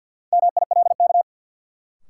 Morse code, heard as a keyed tone, about 48 wpm, 700 Hz. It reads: MILK
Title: Sentences from Top 1500 Words - New Word Drill - 50wpm